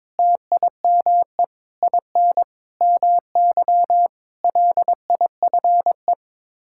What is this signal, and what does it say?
Morse code, heard as a keyed tone, about 22 wpm, 700 Hz. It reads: TIME IN MY LIFE
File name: practice / 22wpm / Sets of 7 Words - 22wpm